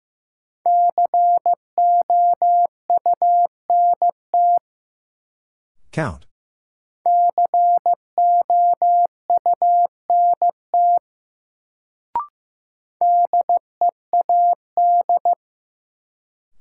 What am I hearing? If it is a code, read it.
Morse code, heard as a keyed tone, about 15 wpm, 700 Hz. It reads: COUNT COUNT DEAD
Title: Sentences from Top 1500 Words - New Word Drill - 15wpm